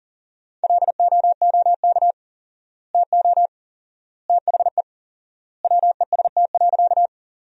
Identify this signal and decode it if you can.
Morse code, heard as a keyed tone, about 40 wpm, 700 Hz. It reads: LOOK TO THE WEST.
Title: 4x Word-Spacing - Sentences from Top 400 Words - 40wpm